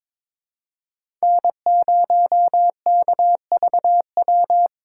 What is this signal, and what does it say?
Morse code, heard as a keyed tone, about 21 wpm, 700 Hz. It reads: N0KVW
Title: USA Callsign DE Callsign - 5 Characters - 22wpm